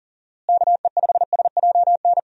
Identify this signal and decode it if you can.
Morse code, heard as a keyed tone, about 40 wpm, 700 Hz. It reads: KE5SJN
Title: USA Call Signs - 6 Characters - Repeated 3x - 40wpm